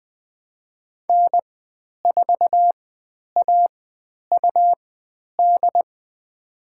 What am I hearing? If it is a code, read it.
Morse code, heard as a keyed tone, about 20 wpm, spaced as for 10 wpm, 700 Hz. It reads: N4AUD
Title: 102 Lesson 17g - Letter u - Call Signs 20wpm